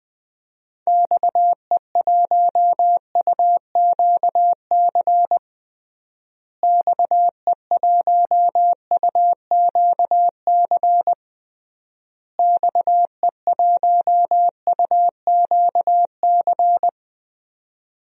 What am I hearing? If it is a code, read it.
Morse code, heard as a keyed tone, about 21 wpm, 700 Hz. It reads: XE1UQC XE1UQC XE1UQC
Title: DXCC-100 Call Signs - 6 Characters - Repeated 3x - 20wpm